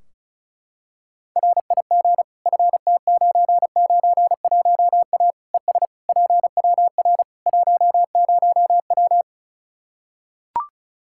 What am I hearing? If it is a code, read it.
Morse code, heard as a keyed tone, about 36 wpm, 700 Hz. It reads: RIG FT991A ES PWR 10W
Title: Practice QSO - Exchange 2 - Info - 35wpm